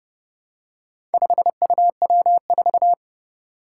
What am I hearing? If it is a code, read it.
Morse code, heard as a keyed tone, about 30 wpm, 700 Hz. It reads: 5UW4
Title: Four Challenging Letter-Number Combinations 30wpm